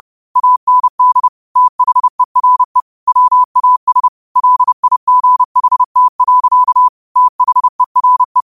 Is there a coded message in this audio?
Yes: Morse code from 0.4 to 8.4 s, heard as a keyed tone, about 30 wpm, 1000 Hz. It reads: AND THERE WAS LIGHT. THERE